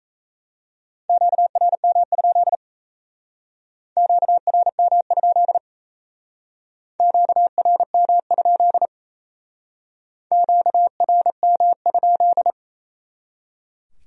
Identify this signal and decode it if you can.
Morse code, heard as a keyed tone, about 37 wpm, 700 Hz. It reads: QRM? QRM? QRM? QRM?